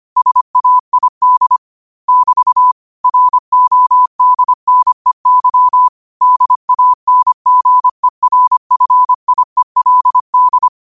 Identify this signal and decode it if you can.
Morse code, heard as a keyed tone, about 25 wpm, 1000 Hz. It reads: SAID <BT> RODNEY DANGERFIELD
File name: 25wpm